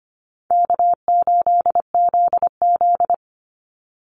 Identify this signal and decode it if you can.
Morse code, heard as a keyed tone, about 25 wpm, 700 Hz. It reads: K8ZZ
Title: Call Signs Repeated 3x - 25wpm